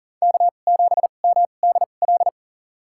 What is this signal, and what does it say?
Morse code, heard as a keyed tone, about 37 wpm, spaced as for 28 wpm, 700 Hz. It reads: K7MDL
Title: USA Call Signs - 5 Characters - 2x Character Spacing - 40wpm